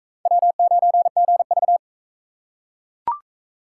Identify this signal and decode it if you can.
Morse code, heard as a keyed tone, about 38 wpm, 700 Hz. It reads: W9GV